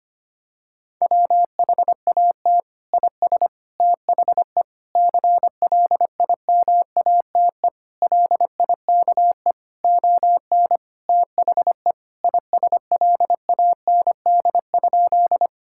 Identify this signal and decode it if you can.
Morse code, heard as a keyed tone, about 25 wpm, 700 Hz. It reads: WHAT IS THE CLIMATE LIKE ON THE ISLAND?
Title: Sentences from Top 1600 Words 25wpm